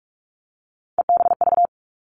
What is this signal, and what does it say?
Morse code, heard as a keyed tone, about 46 wpm, 700 Hz. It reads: EBV
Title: Three Challenging Letter-Number Combinations 45wpm